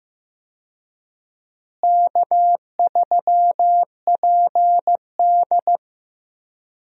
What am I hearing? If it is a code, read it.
Morse code, heard as a keyed tone, about 15 wpm, 700 Hz. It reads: K3PD
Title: USA Call Signs - 4 Characters - Repeated 3x - 15wpm